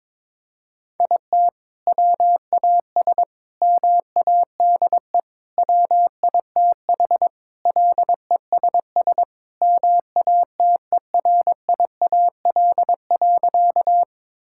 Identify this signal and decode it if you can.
Morse code, heard as a keyed tone, about 22 wpm, 700 Hz. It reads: IT WAS MADE WITH LESS MATERIAL.
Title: Sentences from Top 1000 Words - Review Part 5 - 22wpm